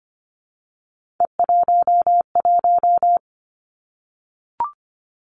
Morse code, heard as a keyed tone, about 25 wpm, 700 Hz. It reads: E11
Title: Three Challenging Letter-Number Combinations 25wpm